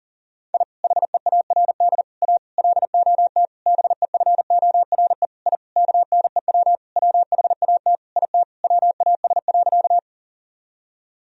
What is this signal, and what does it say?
Morse code, heard as a keyed tone, about 40 wpm, 700 Hz. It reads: I HEARD A LOT BEFORE I KNEW WHAT IT WAS.
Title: Sentences from Top 1000 Words - Review Part 3 - 40wpm